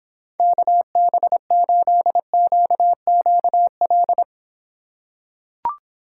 Morse code, heard as a keyed tone, about 26 wpm, 700 Hz. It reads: KB8QQL